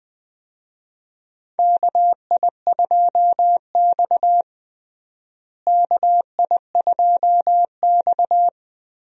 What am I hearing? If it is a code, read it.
Morse code, heard as a keyed tone, about 20 wpm, 700 Hz. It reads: KI2X KI2X
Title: USA Call Signs - 4 Characters - Repeated 3x - 20wpm